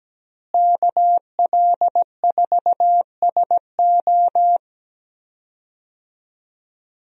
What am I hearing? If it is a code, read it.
Morse code, heard as a keyed tone, about 17 wpm, 700 Hz. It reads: KL4SO
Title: USA Call Signs - 5 Characters - Repeated 3x - 17wpm